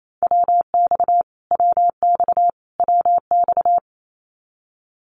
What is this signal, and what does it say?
Morse code, heard as a keyed tone, about 28 wpm, 700 Hz. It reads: WX WX WX
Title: Abbreviations Repeated 3x - 28wpm